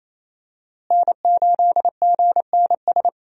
Morse code, heard as a keyed tone, about 28 wpm, 700 Hz. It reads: N8GNS
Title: USA Call Signs - 5 Characters - Repeated 2x - 28wpm